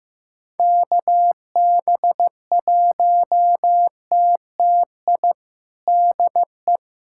Morse code, heard as a keyed tone, about 15 wpm, 700 Hz. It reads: KB1TTI DE
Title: USA Callsign DE Callsign 15wpm